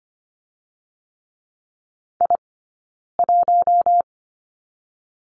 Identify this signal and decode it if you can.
Morse code, heard as a keyed tone, about 24 wpm, 700 Hz. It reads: I 1
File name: Lesson 15d - Number 5 - Two Characters 25wpm